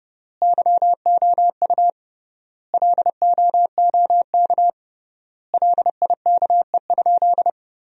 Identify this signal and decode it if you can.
Morse code, heard as a keyed tone, about 32 wpm, 700 Hz. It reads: YOU LOOK LIKE?
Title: 3x Word-Spacing - Sentences from Top 100 Words - 30wpm